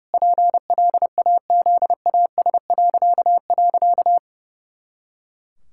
Morse code, heard as a keyed tone, about 30 wpm, 700 Hz. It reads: PLAZAS..
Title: Pangram Challenge v2 - 30wpm